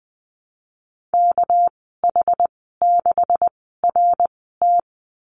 Morse code, heard as a keyed tone, about 20 wpm, spaced as for 14 wpm, 700 Hz. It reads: KH6RT
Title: 245 Lesson 39g - Letter z - Call Signs 20wpm